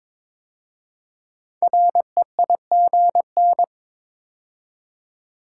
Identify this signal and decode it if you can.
Morse code, heard as a keyed tone, about 22 wpm, 700 Hz. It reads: REIGN
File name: 5 Letter Words - First 3 Letters Given - 22wpm